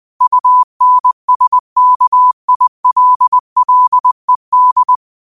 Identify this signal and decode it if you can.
Morse code, heard as a keyed tone, about 20 wpm, 1000 Hz. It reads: UNSKILLED